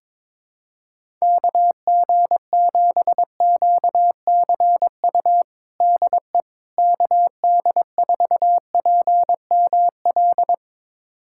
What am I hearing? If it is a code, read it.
Morse code, heard as a keyed tone, about 22 wpm, 700 Hz. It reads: KG7QCU DE KD4PML
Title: USA Callsign DE Callsign - 6 Characters - 22wpm